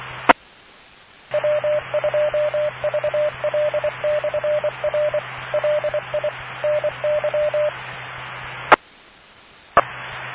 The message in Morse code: E W2VL/R LI NY E E